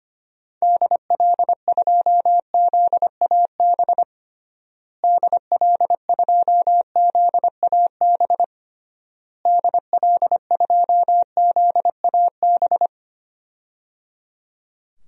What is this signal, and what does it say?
Morse code, heard as a keyed tone, about 25 wpm, 700 Hz. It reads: DL2ZAB DL2ZAB DL2ZAB
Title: DXCC-100 Call Signs - 6 Characters - Repeated 3x - 25wpm